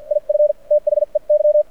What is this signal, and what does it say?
Morse code, heard as a keyed tone, about 52 wpm, spaced as for 42 wpm, 590 Hz. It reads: IF THEY
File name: archived / CRSnet